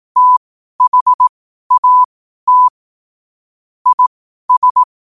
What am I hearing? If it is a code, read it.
Morse code, heard as a keyed tone, about 18 wpm, spaced as for 12 wpm, 1000 Hz. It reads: THAT IS
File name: qotdfiles / 12wpm